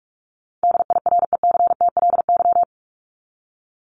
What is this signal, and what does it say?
Morse code, heard as a keyed tone, about 45 wpm, 700 Hz. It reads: DIRECTLY
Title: LY Suffix - Top 100 Words with Base Words 45wpm